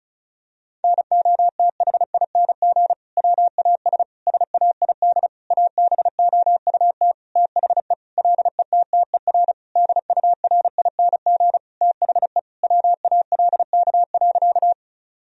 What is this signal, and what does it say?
Morse code, heard as a keyed tone, about 36 wpm, 700 Hz. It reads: NOTHING WAS SAID ABOUT THE LETTER DURING THE WALK.
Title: Sentences from Top 1000 Words - Review Part 2 - 35wpm